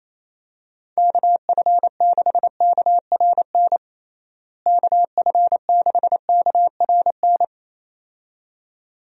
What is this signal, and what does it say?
Morse code, heard as a keyed tone, about 28 wpm, 700 Hz. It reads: KF6KRN KF6KRN
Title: USA Call Signs - 6 Characters - Repeated 2x - 28wpm